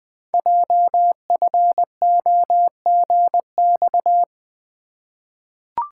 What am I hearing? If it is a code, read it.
Morse code, heard as a keyed tone, about 20 wpm, 700 Hz. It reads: JFOGX
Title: Code Groups - 5 Letters - 20wpm